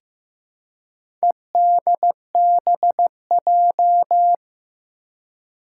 Morse code, heard as a keyed tone, about 15 wpm, 700 Hz. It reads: EDBJ
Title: Four Challenging Letter-Number Combinations 15wpm